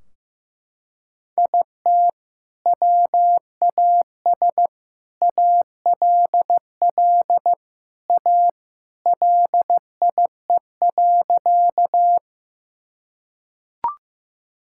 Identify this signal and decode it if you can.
Morse code, heard as a keyed tone, about 15 wpm, 700 Hz. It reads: IT WAS ALL A LIE.
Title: Sentences from Top 1600 Words 15wpm